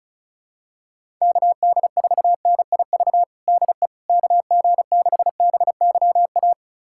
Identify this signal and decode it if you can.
Morse code, heard as a keyed tone, about 36 wpm, 700 Hz. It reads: KD4NIV DE KG6BYA
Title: USA Callsign DE Callsign 35wpm